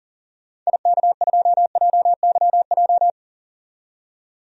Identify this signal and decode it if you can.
Morse code, heard as a keyed tone, about 40 wpm, 700 Hz. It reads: IK2JYJ
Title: DXCC-100 Call Signs - 6 Characters - 40wpm